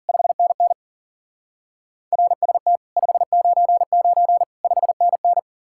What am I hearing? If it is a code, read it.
Morse code, heard as a keyed tone, about 48 wpm, 700 Hz. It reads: 5NN RST 599 5NN
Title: Speed-Racing Common QSO Phrases - 40wpm